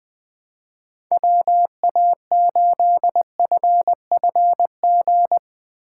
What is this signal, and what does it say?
Morse code, heard as a keyed tone, about 21 wpm, 700 Hz. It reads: WA8FFG